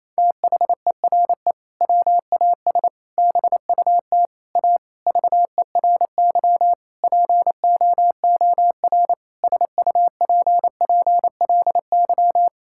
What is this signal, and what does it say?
Morse code, heard as a keyed tone, about 28 wpm, 700 Hz. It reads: THERE WAS BUT A VERY POOR SUPPLY
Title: Sentences from Top 1000 Words - Review Part 5 - 28wpm